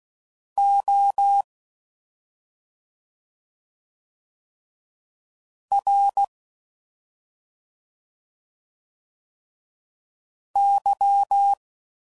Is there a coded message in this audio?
Yes: Morse code from 0.6 to 11.5 s, heard as a keyed tone, about 16 wpm, 800 Hz. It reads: O R Y